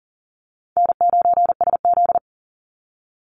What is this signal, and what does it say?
Morse code, heard as a keyed tone, about 37 wpm, 700 Hz. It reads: N9SZ
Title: Easy Call Signs - 40wpm